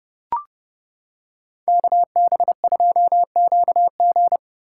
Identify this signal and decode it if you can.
Morse code, heard as a keyed tone, about 29 wpm, 700 Hz. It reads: KB2QG